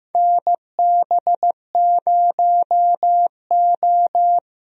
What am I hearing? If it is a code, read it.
Morse code, heard as a keyed tone, about 15 wpm, 700 Hz. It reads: NB0O
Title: USA Call Signs - 4 Characters - Repeated 2x - 15wpm